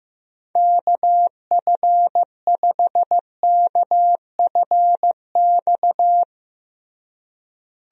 Morse code, heard as a keyed tone, about 15 wpm, 700 Hz. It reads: KF5KFX
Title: USA Call Signs - 6 Characters - Repeated 3x - 15wpm